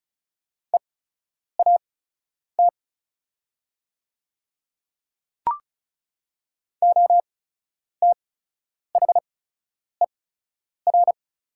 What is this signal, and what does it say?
Morse code, heard as a keyed tone, about 36 wpm, spaced as for 10 wpm, 700 Hz. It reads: EAT OTHER
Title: 8x Character Spacing - Top 300 Words - 35wpm